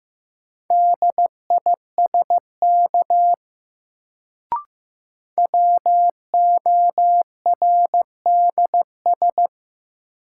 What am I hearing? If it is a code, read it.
Morse code, heard as a keyed tone, about 15 wpm, 700 Hz. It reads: DISK WORDS
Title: Sentences from Top 1000 Words - Review Part 6 - New Word Drill - 15wpm